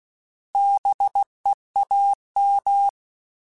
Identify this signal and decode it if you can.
Morse code, heard as a keyed tone, about 16 wpm, 800 Hz. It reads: BEAM
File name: Alphabetical / Alphabetical mp3s / Bs @15